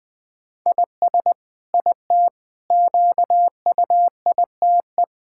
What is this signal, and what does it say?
Morse code, heard as a keyed tone, about 20 wpm, 700 Hz. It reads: IS IT QUITE
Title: Sentences from Top 1000 Words - Review Part 6 - 20wpm